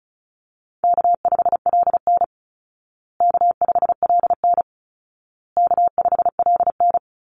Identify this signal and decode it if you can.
Morse code, heard as a keyed tone, about 34 wpm, 700 Hz. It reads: K5LN K5LN K5LN
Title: Easy Call Signs Repeated 3x - 35wpm